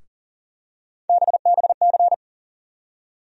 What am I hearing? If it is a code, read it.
Morse code, heard as a keyed tone, about 39 wpm, 700 Hz. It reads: BBC
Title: USA Call Sign Postfixes - 3 Letters 40wpm